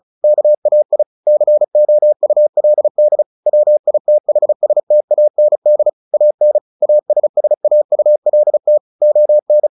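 Morse code, heard as a keyed tone, about 34 wpm, 600 Hz. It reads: KAI COULD WITHSTAND AN ASSAULT ON
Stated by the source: Morse practice at 35 wpm from CW med Gnister, sending Ken Follett's novel Never in English.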